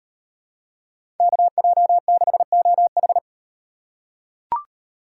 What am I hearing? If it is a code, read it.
Morse code, heard as a keyed tone, about 34 wpm, 700 Hz. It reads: KJ6OH